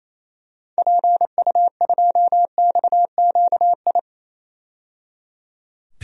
Morse code, heard as a keyed tone, about 27 wpm, 700 Hz. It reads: PU2XQI
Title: DXCC-100 Call Signs - 6 Characters - 28wpm